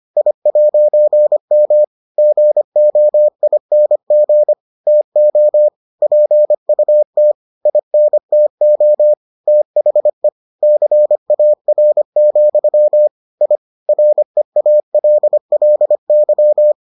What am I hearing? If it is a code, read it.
Morse code, heard as a keyed tone, about 25 wpm, 600 Hz. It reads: I'M GOING TO PUT INTO THE CAR, I REALLY